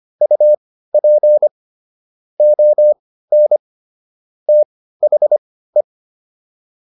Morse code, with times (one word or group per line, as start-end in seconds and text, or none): 0.2-1.5 UP
2.4-3.6 ON
4.5-5.8 THE